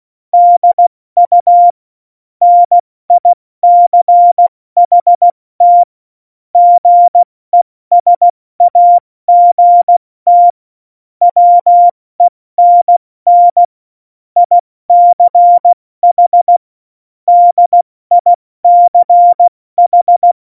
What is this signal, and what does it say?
Morse code, heard as a keyed tone, about 16 wpm, 700 Hz. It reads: DU NICHT GESAGT WENN ICH DICH
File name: Morsekurs / Witze / Witz 18. September 2018, 70 F80